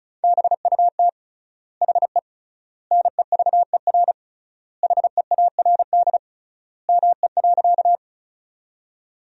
Morse code, heard as a keyed tone, about 36 wpm, 700 Hz. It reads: BUT HE NEVER HEARD ME.
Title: 3x Word-Spacing - Sentences from Top 400 Words - 35wpm